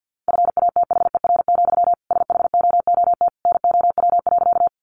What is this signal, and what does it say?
Morse code, heard as a keyed tone, about 50 wpm, 700 Hz. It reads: FATHER, SHOOT NOW.
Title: Sentences from Top 1200 Words 50wpm